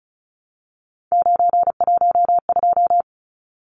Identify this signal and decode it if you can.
Morse code, heard as a keyed tone, about 34 wpm, 700 Hz. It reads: 912
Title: Numbers - 3-Digit 35wpm